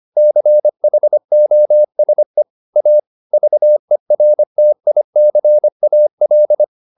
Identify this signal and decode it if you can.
Morse code, heard as a keyed tone, about 25 wpm, 600 Hz. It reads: CHOSE A VERTICAL
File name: SS202 25